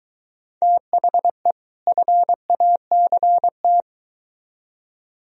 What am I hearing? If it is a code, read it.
Morse code, heard as a keyed tone, about 22 wpm, 700 Hz. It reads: THE FACT